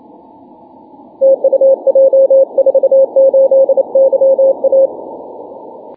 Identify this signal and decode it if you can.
Morse code, heard as a keyed tone, about 28 wpm, 540 Hz. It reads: TU J48YA
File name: wp / j48ya 6m cw 1 jul 2011